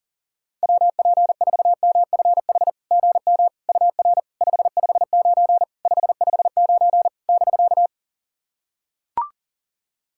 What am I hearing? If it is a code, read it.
Morse code, heard as a keyed tone, about 37 wpm, 700 Hz. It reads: WP4MFH GM UR 559 559 <BK>
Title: SOTA QSO Practice 40wpm